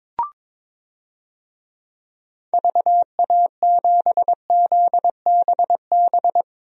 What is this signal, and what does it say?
Morse code, heard as a keyed tone, about 22 wpm, 700 Hz. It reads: VA7ZBB
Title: DXCC-100 Call Signs - 6 Characters - Repeated 2x - 22wpm